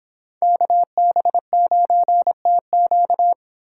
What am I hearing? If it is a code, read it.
Morse code, heard as a keyed tone, about 24 wpm, 700 Hz. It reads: KB9TQ